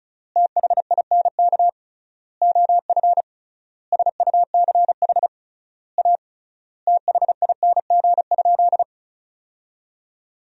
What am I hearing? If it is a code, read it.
Morse code, heard as a keyed tone, about 34 wpm, 700 Hz. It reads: THINK OF SUCH A THING?
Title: 3x Word-Spacing - Sentences from Top 200 Words - 35wpm